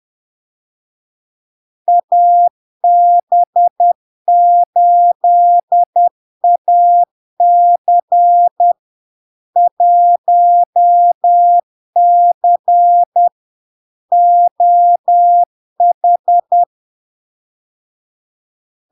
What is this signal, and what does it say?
Morse code, heard as a keyed tone, about 10 wpm, 700 Hz. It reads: AB8AC 1C OH
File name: LARK-FD-2022-Log-in-CW-10wpm-700-Hz